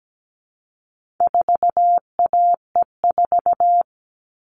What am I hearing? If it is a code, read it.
Morse code, heard as a keyed tone, about 17 wpm, 700 Hz. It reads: 4AE4
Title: All Cut Numbers - 4-Digit 17wpm